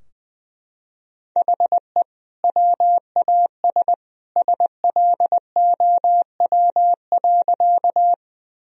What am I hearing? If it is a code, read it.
Morse code, heard as a keyed tone, about 20 wpm, 700 Hz. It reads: HE WAS SLOW.
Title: Sentences from Top 1000 Words - Review Part 2 - 20wpm